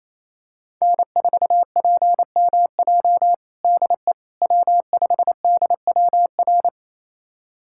Morse code, heard as a keyed tone, about 28 wpm, 700 Hz. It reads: N4PMJ DE W5DWR
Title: USA Callsign DE Callsign - 5 Characters - 28wpm